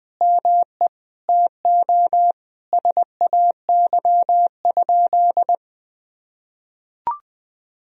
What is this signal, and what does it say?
Morse code, heard as a keyed tone, about 20 wpm, 700 Hz. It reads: ME TO SAY?
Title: Sentences from Top 200 Words and Abbreviations 20wpm